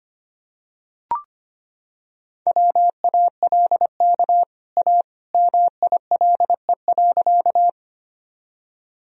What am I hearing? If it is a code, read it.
Morse code, heard as a keyed tone, about 25 wpm, 700 Hz. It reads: WALK A MILE.
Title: Sentences from Top 1000 Words - Review Part 1 - 25wpm